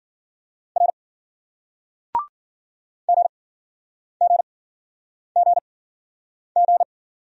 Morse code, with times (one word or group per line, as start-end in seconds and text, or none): 0.8-0.9 R
3.1-6.8 GGGG